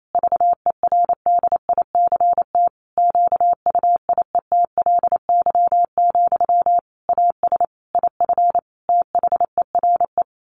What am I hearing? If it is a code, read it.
Morse code, heard as a keyed tone, about 28 wpm, 700 Hz. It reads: VERDICT QUIETLY, AS IF THERE
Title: A Sound Of Thunder 28wpm